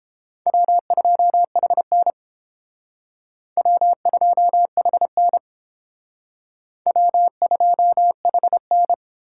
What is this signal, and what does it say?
Morse code, heard as a keyed tone, about 31 wpm, 700 Hz. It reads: W2HN W2HN W2HN